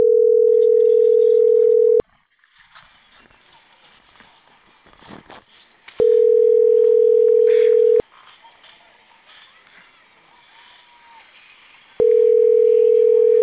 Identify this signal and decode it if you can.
Ringback tone (North America), 440 + 480 Hz.